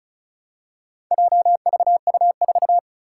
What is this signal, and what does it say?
Morse code, heard as a keyed tone, about 36 wpm, 690 Hz. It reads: JVU4
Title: Four Challenging Letter-Number Combinations 35wpm